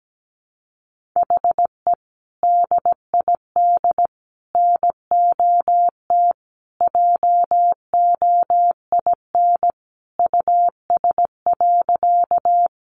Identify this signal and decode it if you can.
Morse code, heard as a keyed tone, about 17 wpm, 700 Hz. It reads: HE DID NOT JOIN US.
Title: Sentences from Top 1400 Words 17wpm